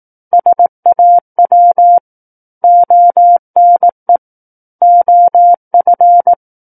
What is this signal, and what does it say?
Morse code, heard as a keyed tone, about 18 wpm, 700 Hz. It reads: SAW ONE OF
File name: Book One / 09-Chapter 9 - 18 WPM